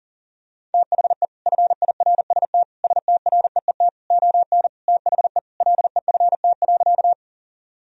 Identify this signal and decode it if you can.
Morse code, heard as a keyed tone, about 40 wpm, 700 Hz. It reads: THE FIRST STREET ON THE LEFT.
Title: Sentences from Top 1000 Words - Review Part 3 - 40wpm